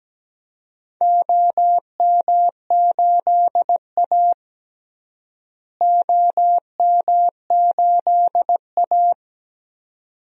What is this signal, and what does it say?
Morse code, heard as a keyed tone, about 17 wpm, 700 Hz. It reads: OM8A OM8A
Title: DXCC-100 Call Signs - 4 Characters - Repeated 2x - 17wpm